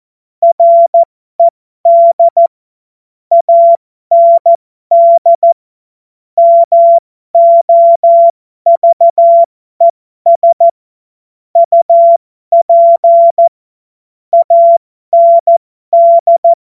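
Morse code, heard as a keyed tone, about 14 wpm, spaced as for 12 wpm, 680 Hz. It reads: RED AND MOVES UP AND
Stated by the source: Morse joke practice at effective 60, Farnsworth 70.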